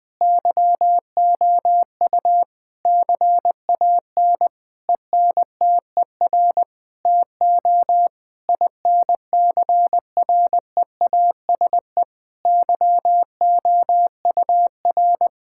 Morse code, heard as a keyed tone, about 21 wpm, 700 Hz. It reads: YOU CAN ENTER TO INCREASE YOUR
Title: Sentences with 8-Word Phrases - 20wpm